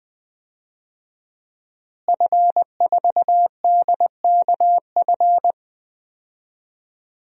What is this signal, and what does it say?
Morse code, heard as a keyed tone, about 20 wpm, 700 Hz. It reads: F4DKF
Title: DXCC-100 Call Signs - 5 Characters - Repeated 2x - 20wpm